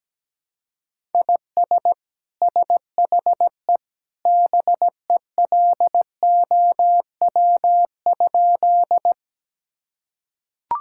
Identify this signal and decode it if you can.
Morse code, heard as a keyed tone, about 17 wpm, 700 Hz. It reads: IS SHE BELOW?
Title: Sentences from Top 1000 Words - Review Part 5 - 17wpm